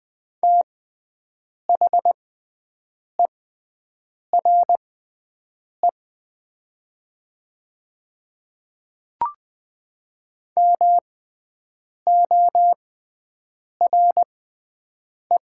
Morse code, heard as a keyed tone, about 20 wpm, spaced as for 7 wpm, 700 Hz. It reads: THERE MORE
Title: 6x Character Spacing - Top 100 Words - 20wpm